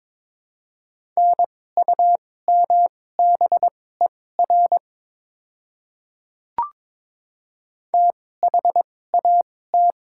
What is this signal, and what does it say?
Morse code, heard as a keyed tone, about 22 wpm, spaced as for 16 wpm, 700 Hz. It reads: NUMBER THAT
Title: 2x Character Spacing - Top 100 Words - 22wpm